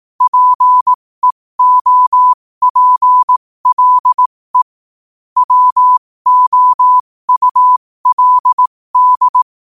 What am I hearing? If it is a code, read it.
Morse code, heard as a keyed tone, about 18 wpm, spaced as for 15 wpm, 1000 Hz. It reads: PEOPLE WOULD
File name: qotdfiles / 15wpm